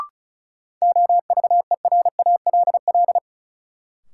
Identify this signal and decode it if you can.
Morse code, heard as a keyed tone, about 34 wpm, 700 Hz. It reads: OVERALL
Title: Sentences from Top 1000 Words - Review Part 7 - New Word Drill - 35wpm